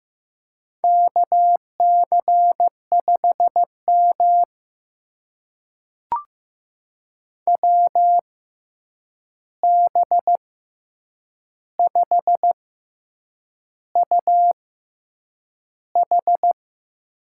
Morse code, heard as a keyed tone, about 15 wpm, spaced as for 7 wpm, 700 Hz. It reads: KC5M WB5UH
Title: USA Call Signs - 6x Character Spacing - 15wpm